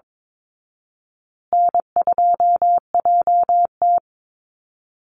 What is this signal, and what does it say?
Morse code, heard as a keyed tone, about 22 wpm, 700 Hz. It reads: N2JT
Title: Easy Call Signs Repeated 2x - 22wpm